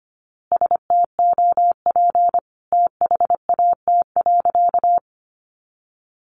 Morse code, heard as a keyed tone, about 25 wpm, 700 Hz. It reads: STOP THAT.
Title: A Sound Of Thunder 25wpm